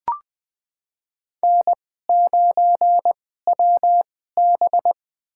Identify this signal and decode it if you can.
Morse code, heard as a keyed tone, about 20 wpm, spaced as for 14 wpm, 700 Hz. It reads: N9WB
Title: USA Call Signs - 4 Characters - 2x Character Spacing - 20wpm